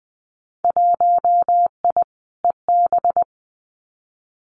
Morse code, heard as a keyed tone, about 20 wpm, 700 Hz. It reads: 1I EB
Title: Winter Field Day Exchanges Encore 20wpm